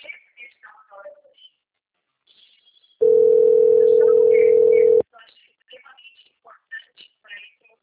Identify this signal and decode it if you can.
Ringback tone (North America), 440 + 480 Hz, with speech.
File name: out- - - - - .4460